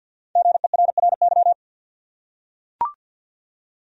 Morse code, heard as a keyed tone, about 45 wpm, 700 Hz. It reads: GERRY
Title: USA Common Names - Max 5 Letters - 50wpm